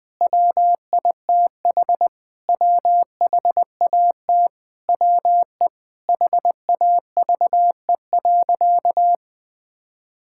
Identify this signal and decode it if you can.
Morse code, heard as a keyed tone, about 21 wpm, 700 Hz. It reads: WITH WHAT WE HAVE.
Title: Sentences from Top 1000 Words - Review Part 6 - 20wpm